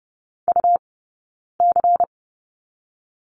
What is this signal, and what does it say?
Morse code, heard as a keyed tone, about 30 wpm, 700 Hz. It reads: U C
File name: Lesson 18d - Letter m - Two Characters 30wpm